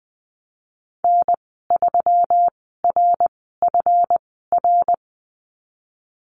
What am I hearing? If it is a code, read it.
Morse code, heard as a keyed tone, about 20 wpm, spaced as for 15 wpm, 700 Hz. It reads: N3RFR
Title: 147 Lesson 24g - Letter y - Call Signs 20wpm